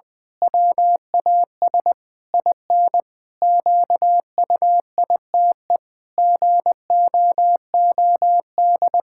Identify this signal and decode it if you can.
Morse code, heard as a keyed tone, about 21 wpm, 700 Hz. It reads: WAS IN QUITE GOOD
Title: Pangram Challenge v2 - 20wpm